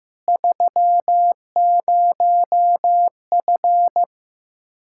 Morse code, heard as a keyed tone, about 15 wpm, 700 Hz. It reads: 30F
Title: K1USN SST QSO Practice 15wpm